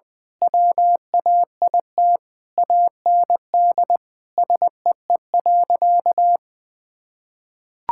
Morse code, heard as a keyed tone, about 20 wpm, 700 Hz. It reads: WAIT AND SEE.
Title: Sentences with 7-Word Phrases - 20wpm